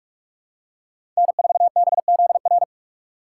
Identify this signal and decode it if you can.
Morse code, heard as a keyed tone, about 47 wpm, 700 Hz. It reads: N4BZR
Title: USA Call Signs - 5 Characters - 4x Character Spacing - 45wpm